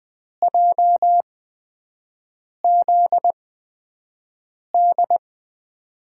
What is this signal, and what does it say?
Morse code, heard as a keyed tone, about 20 wpm, spaced as for 5 wpm, 700 Hz. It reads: JZD
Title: USA Call Signs - 5 Characters - 8x Character Spacing - 20wpm